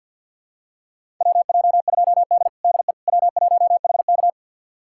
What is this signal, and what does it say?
Morse code, heard as a keyed tone, about 53 wpm, 700 Hz. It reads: WJ2D DE W1HK
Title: USA Callsign DE Callsign - 4 Characters - 50wpm